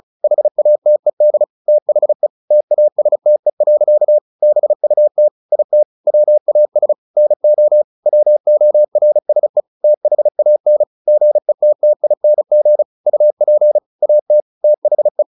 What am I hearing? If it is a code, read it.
Morse code, heard as a keyed tone, about 35 wpm, 600 Hz. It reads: HATED THE TASTE. BUT IT WAS NO WORSE THAN GETTING UP AT THE
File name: the old man and the sea / the old man and the sea 1 35wpm